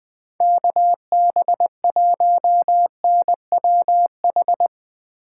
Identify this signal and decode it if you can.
Morse code, heard as a keyed tone, about 20 wpm, 700 Hz. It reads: KB1NWH
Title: USA Call Signs - 6 Characters - 2x Character Spacing - 20wpm